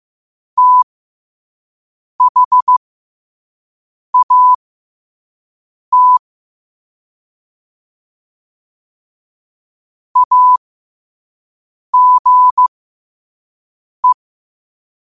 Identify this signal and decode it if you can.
Morse code, heard as a keyed tone, about 15 wpm, spaced as for 5 wpm, 1000 Hz. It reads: THAT AGE
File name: 5wpm